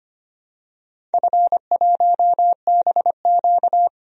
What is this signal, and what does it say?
Morse code, heard as a keyed tone, about 25 wpm, 700 Hz. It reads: F1BQ
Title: DXCC-100 Call Signs - 4 Characters - Repeated 2x - 25wpm